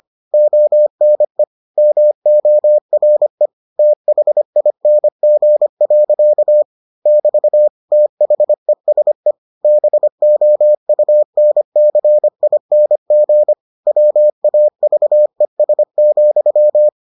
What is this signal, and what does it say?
Morse code, heard as a keyed tone, about 25 wpm, 600 Hz. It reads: ONE MORE THING. <BT> THESE BOUNCING WAVES,